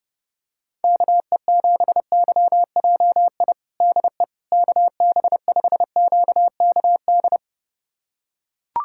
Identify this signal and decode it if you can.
Morse code, heard as a keyed tone, about 29 wpm, 700 Hz. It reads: KE7YJI DE KB5QKD
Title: USA Callsign DE Callsign - 6 Characters - 30wpm